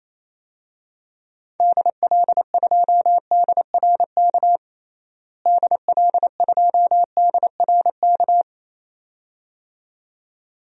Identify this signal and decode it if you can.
Morse code, heard as a keyed tone, about 28 wpm, 700 Hz. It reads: DL2DRK DL2DRK
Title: DXCC-100 Call Signs - 6 Characters - Repeated 2x - 28wpm